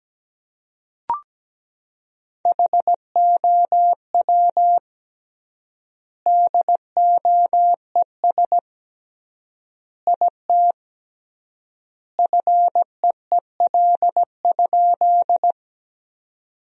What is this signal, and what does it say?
Morse code, heard as a keyed tone, about 17 wpm, 700 Hz. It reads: HOW DOES IT FEEL?
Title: 3x Word-Spacing - Sentences from Top 400 Words - 17wpm